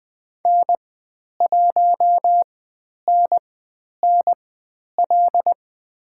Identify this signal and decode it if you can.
Morse code, heard as a keyed tone, about 20 wpm, spaced as for 10 wpm, 700 Hz. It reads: N1NNL
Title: 095 Lesson 16g - Letter c - Call Signs 20wpm